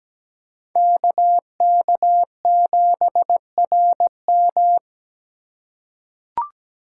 Morse code, heard as a keyed tone, about 17 wpm, 700 Hz. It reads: KK7RM
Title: USA Call Signs - 5 Characters - 4x Character Spacing - 17wpm